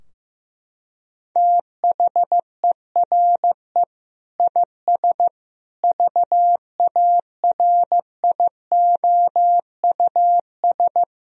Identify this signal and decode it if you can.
Morse code, heard as a keyed tone, about 16 wpm, 700 Hz. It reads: THERE IS VARIOUS
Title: Sentences from Top 1000 Words - Review Part 4 - 15wpm